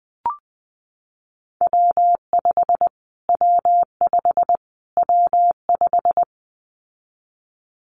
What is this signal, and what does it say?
Morse code, heard as a keyed tone, about 20 wpm, 700 Hz. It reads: W5 W5 W5
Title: Callsign Prefixes for Top 25 DXCC - Repeated 3x - 20wpm